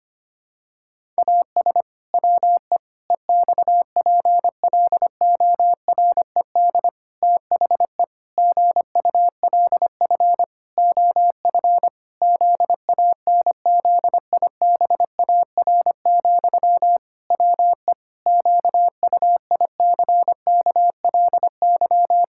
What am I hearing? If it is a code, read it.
Morse code, heard as a keyed tone, about 25 wpm, 700 Hz. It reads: AS WE EXPLORED THE GULF OF ZANZIBAR, WE QUICKLY
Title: Pangram Challenge v2 - 25wpm